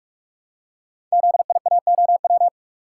Morse code, heard as a keyed tone, about 44 wpm, 690 Hz. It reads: ZIAOW
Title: Code Groups - 5 Letter:Numbers - 45wpm